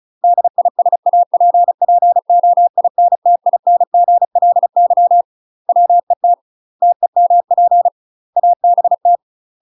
Morse code, heard as a keyed tone, about 36 wpm, 700 Hz. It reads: DISAPPOINTINGLY WET TEMP ABT
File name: UK MASTER QSOs TRAINING / 35WPM / UK MASTER QSOs 0043